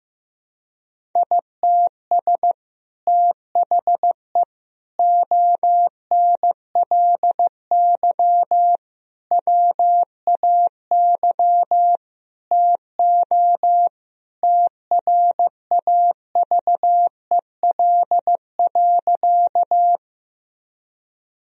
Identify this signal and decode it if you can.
Morse code, heard as a keyed tone, about 15 wpm, 700 Hz. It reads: ITS THE ONLY WAY TO TRAVEL.
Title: Sentences from Top 1000 Words - Review Part 2 - 15wpm